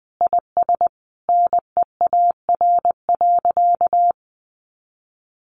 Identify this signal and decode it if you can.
Morse code, heard as a keyed tone, about 20 wpm, 700 Hz. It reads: IS NEAR.
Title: Sentences from Top 700 Words 20wpm